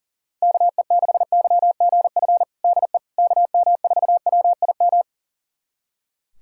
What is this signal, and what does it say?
Morse code, heard as a keyed tone, about 40 wpm, 700 Hz. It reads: KE6YGF DE KM4WIM
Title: USA Callsign DE Callsign - 6 Characters - 40wpm